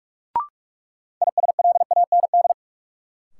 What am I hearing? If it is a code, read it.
Morse code, heard as a keyed tone, about 41 wpm, 700 Hz. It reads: ISLAND
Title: Sentences from Top 500 Words - New Word Drill - 45wpm